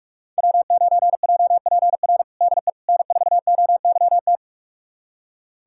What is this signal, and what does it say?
Morse code, heard as a keyed tone, about 44 wpm, 700 Hz. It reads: W9JPR DE N4OYT
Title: USA Callsign DE Callsign - 5 Characters - 45wpm